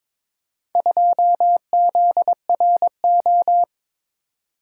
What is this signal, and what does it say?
Morse code, heard as a keyed tone, about 22 wpm, 700 Hz. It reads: 2ZRO
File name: practice / 22wpm / Four Letter-Number Combinations - 22wpm